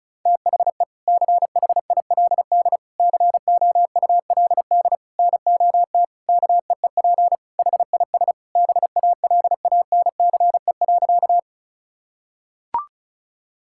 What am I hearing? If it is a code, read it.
Morse code, heard as a keyed tone, about 36 wpm, 700 Hz. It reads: THE CHILD COULD NOT KEEP HIS BALANCE.
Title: Sentences from Top 1000 Words - Review Part 6 - 35wpm